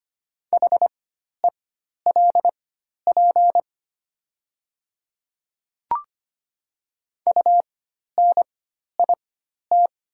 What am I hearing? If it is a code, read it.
Morse code, heard as a keyed tone, about 25 wpm, spaced as for 12 wpm, 700 Hz. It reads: HELP UNIT
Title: 4x Character Spacing - Top 500 Words - 25wpm